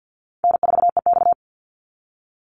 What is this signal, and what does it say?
Morse code, heard as a keyed tone, about 50 wpm, 700 Hz. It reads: N4EX
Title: NA SOTA Activator Callsigns 50wpm